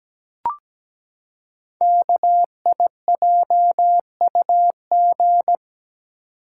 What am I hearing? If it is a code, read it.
Morse code, heard as a keyed tone, about 17 wpm, 700 Hz. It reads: KIJUG
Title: Code Groups - 5 Letters - 17wpm